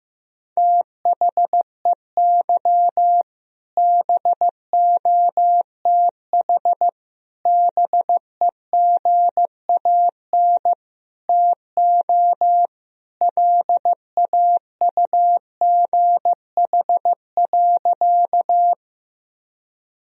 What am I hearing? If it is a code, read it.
Morse code, heard as a keyed tone, about 15 wpm, 700 Hz. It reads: THEY BOTH BEGAN TO LAUGH.
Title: Sentences from Top 1000 Words - Review Part 1 - 15wpm